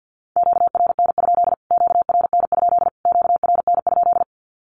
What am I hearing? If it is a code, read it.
Morse code, heard as a keyed tone, about 49 wpm, 700 Hz. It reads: QRN? QRN? QRN?
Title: Q-Codes Repeated 3x 50wpm